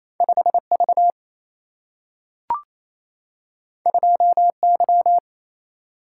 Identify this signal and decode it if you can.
Morse code, heard as a keyed tone, about 28 wpm, 700 Hz. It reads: 5 V 2 Y
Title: Single-Letter Single-Number 28wpm